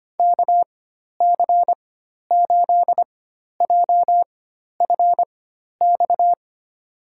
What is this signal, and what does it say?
Morse code, heard as a keyed tone, about 25 wpm, spaced as for 12 wpm, 700 Hz. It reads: KC8JFX
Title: USA Call Signs - 4x Character Spacing - 25wpm